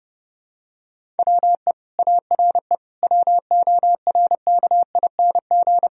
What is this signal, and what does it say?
Morse code, heard as a keyed tone, about 30 wpm, 700 Hz. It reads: WE ARE WORKING
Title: Sentences from Top 1000 Words - Review Part 4 - 30wpm